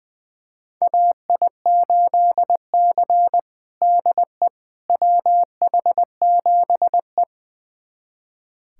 Morse code, heard as a keyed tone, about 20 wpm, 700 Hz. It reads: AI8C DE WH7E
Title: USA Callsign DE Callsign - 4 Characters - 20wpm